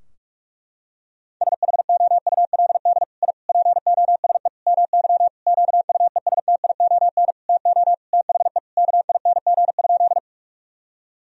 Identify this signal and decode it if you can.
Morse code, heard as a keyed tone, about 46 wpm, 700 Hz. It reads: SHOULD I POSE MY QUESTION TO THE KING?
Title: Sentences from Top 1000 Words - Review Part 2 - 45wpm